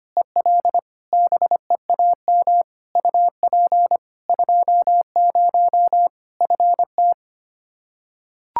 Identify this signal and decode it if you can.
Morse code, heard as a keyed tone, about 25 wpm, 700 Hz. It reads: EL BEAM UP 20 FT
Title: Practice QSO - Exchange 2 - Info - 25wpm